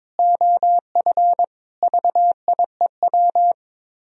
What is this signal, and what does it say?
Morse code, heard as a keyed tone, about 22 wpm, 700 Hz. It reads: OF VIEW
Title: Speed-Racing - Sets of 5 Words 22wpm